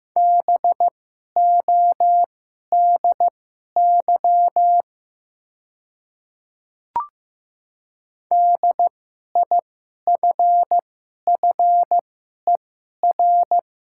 Morse code, heard as a keyed tone, about 15 wpm, spaced as for 11 wpm, 700 Hz. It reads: BODY DIFFER
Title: 2x Character Spacing - Top 500 Words - 15wpm